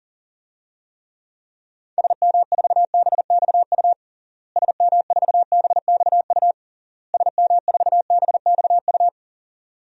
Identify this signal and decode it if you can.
Morse code, heard as a keyed tone, about 40 wpm, 700 Hz. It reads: SM4BXU SM4BXU SM4BXU
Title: DXCC-100 Call Signs - 6 Characters - Repeated 3x - 40wpm